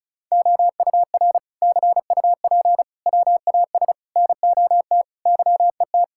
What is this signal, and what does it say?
Morse code, heard as a keyed tone, about 36 wpm, 690 Hz. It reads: OUR CUP WAS NOT YET
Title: Sentences from Top 1600 Words 35wpm